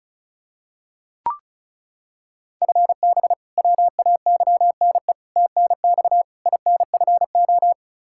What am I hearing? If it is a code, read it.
Morse code, heard as a keyed tone, about 34 wpm, 700 Hz. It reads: FB WAYNE TNX INFO
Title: K1USN SST QSO Practice 35wpm